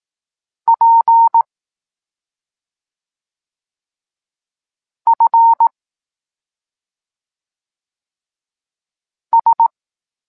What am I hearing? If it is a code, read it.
Morse code, heard as a keyed tone, about 18 wpm, 920 Hz. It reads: P F S
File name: Tekst-26 1